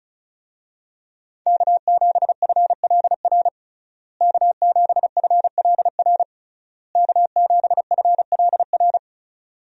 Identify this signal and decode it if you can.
Morse code, heard as a keyed tone, about 34 wpm, 700 Hz. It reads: K7FLR K7FLR K7FLR
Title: USA Call Signs - 5 Characters - Repeated 3x - 35wpm